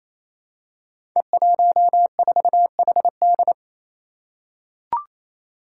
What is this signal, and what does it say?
Morse code, heard as a keyed tone, about 29 wpm, 700 Hz. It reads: E14HD
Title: Five Challenging Letter-Number Combinations 28wpm